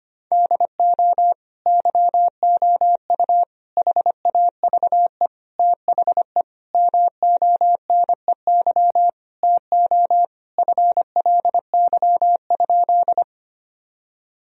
Morse code, heard as a keyed tone, about 25 wpm, 700 Hz. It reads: DO YOU HAVE THE MONEY TO FLY?
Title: Sentences from Top 1000 Words - Review Part 2 - 25wpm